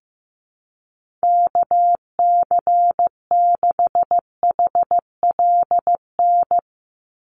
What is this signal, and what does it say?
Morse code, heard as a keyed tone, about 15 wpm, 700 Hz. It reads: KC6HLN
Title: USA Call Signs 15wpm